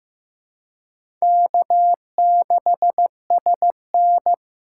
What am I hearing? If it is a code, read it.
Morse code, heard as a keyed tone, about 15 wpm, 700 Hz. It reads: K6SN
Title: USA Call Signs - 4 Characters - 3x Character Spacing - 15wpm